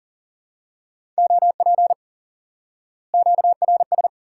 Morse code, heard as a keyed tone, about 40 wpm, 700 Hz. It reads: OP QRS
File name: IQE Course 3 - Lesson 89 - YAGI - Two Elements - 40wpm